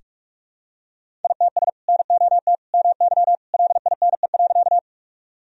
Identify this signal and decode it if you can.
Morse code, heard as a keyed tone, about 46 wpm, 700 Hz. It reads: ITS NOT MY LINE.
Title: Sentences from Top 1000 Words - Review Part 1 - 45wpm